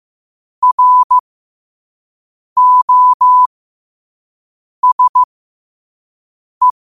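Morse code, heard as a keyed tone, about 15 wpm, spaced as for 5 wpm, 1000 Hz. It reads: ROSE
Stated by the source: Morse code at 5 wpm.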